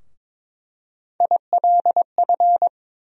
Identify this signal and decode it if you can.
Morse code, heard as a keyed tone, about 22 wpm, 700 Hz. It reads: ILF
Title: USA Call Sign Postfixes - 3 Letters 22wpm